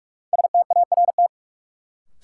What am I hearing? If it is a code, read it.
Morse code, heard as a keyed tone, about 44 wpm, 700 Hz. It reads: START
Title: Sentences from Top 300 Words - New Word Drill - 45wpm